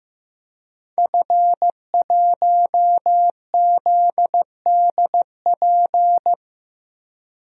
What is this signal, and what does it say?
Morse code, heard as a keyed tone, about 15 wpm, 700 Hz. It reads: F1ZDP
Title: DXCC-100 Call Signs - 5 Characters - Repeated 3x - 15wpm